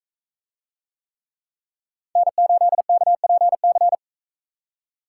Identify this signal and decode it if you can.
Morse code, heard as a keyed tone, about 42 wpm, 700 Hz. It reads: N8KPC